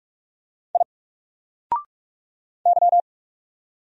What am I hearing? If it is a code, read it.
Morse code, heard as a keyed tone, about 44 wpm, 700 Hz. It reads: I Y
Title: Single Letter-Number Alt 45wpm